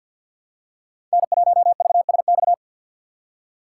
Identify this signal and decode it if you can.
Morse code, heard as a keyed tone, about 52 wpm, 700 Hz. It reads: N1VSX
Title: USA Call Signs - 5 Characters - 3x Character Spacing - 50wpm